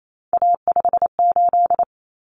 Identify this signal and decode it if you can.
Morse code, heard as a keyed tone, about 28 wpm, 700 Hz. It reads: A58
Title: Common Cut Numbers - 3-Digit 28wpm